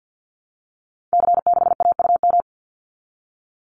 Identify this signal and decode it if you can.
Morse code, heard as a keyed tone, about 49 wpm, 700 Hz. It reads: C6AUM
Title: Call Signs Repeated 3x - 50wpm